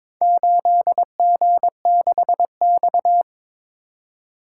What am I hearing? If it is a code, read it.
Morse code, heard as a keyed tone, about 22 wpm, 700 Hz. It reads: 8G6X
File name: practice / 22wpm / Four Letter-Number Combinations - 22wpm